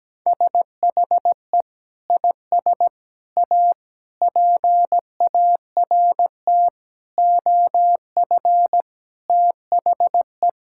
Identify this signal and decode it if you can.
Morse code, heard as a keyed tone, about 17 wpm, 700 Hz. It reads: SHE IS A PART OF THE
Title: Sentences from Top 1000 Words - Review Part 5 - 17wpm